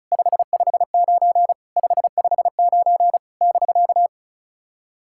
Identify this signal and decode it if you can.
Morse code, heard as a keyed tone, about 36 wpm, 690 Hz. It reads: 559 559 <BK>
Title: SOTA QSO Practice 35wpm